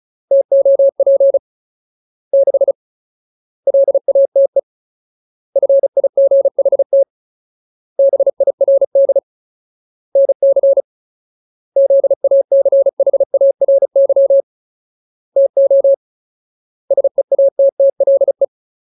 Morse code, heard as a keyed tone, about 34 wpm, 550 Hz. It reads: TOP 6 LATE FIGHT BIRD NC ZACHARY TO SEATTLE
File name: Ultimate Word List / 35WPM Ultimate